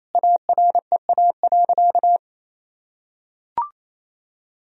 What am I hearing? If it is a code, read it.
Morse code, heard as a keyed tone, about 27 wpm, 700 Hz. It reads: AREA.
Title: Sentences from Top 1000 Words - Review Part 2 - 28wpm